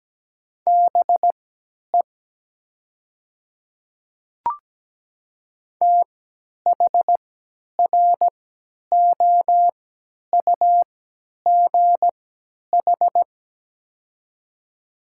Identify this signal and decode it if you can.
Morse code, heard as a keyed tone, about 17 wpm, spaced as for 10 wpm, 690 Hz. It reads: BE THROUGH
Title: 3x Character Spacing - Top 200 Words - 17wpm